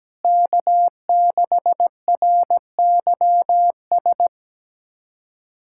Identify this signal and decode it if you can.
Morse code, heard as a keyed tone, about 17 wpm, 700 Hz. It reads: K6RYS
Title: USA Callsign DE Callsign - 5 Characters - 17wpm